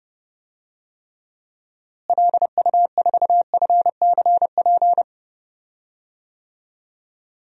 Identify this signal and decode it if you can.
Morse code, heard as a keyed tone, about 32 wpm, 700 Hz. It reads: LU4FCP
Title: DXCC-100 Call Signs - 6 Characters - Repeated 2x - 30wpm